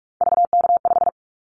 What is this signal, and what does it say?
Morse code, heard as a keyed tone, about 44 wpm, 700 Hz. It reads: VK5
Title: Callsign Prefixes for Top 100 DXCC - 45wpm